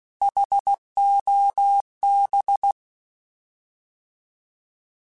Morse code, heard as a keyed tone, about 16 wpm, 800 Hz. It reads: HOB
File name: Alphabetical / Alphabetical mp3s / Hs @15